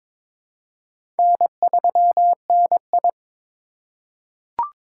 Morse code, heard as a keyed tone, about 22 wpm, 700 Hz. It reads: N3NI
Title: USA Call Signs - 4x Character Spacing - 22wpm